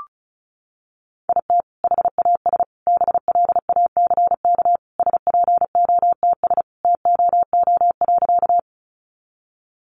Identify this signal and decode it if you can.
Morse code, heard as a keyed tone, about 34 wpm, 700 Hz. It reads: IT HAS BLACK SPOTS TOO.
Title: Additional Easy Phrases 35wpm